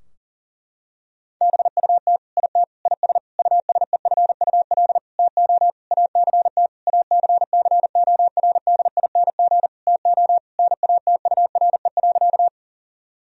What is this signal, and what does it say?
Morse code, heard as a keyed tone, about 40 wpm, 700 Hz. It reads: BUT IT IS USEFUL TO ACT ACCORDING TO NATURE.
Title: Sentences from Top 1000 Words - Review Part 4 - 40wpm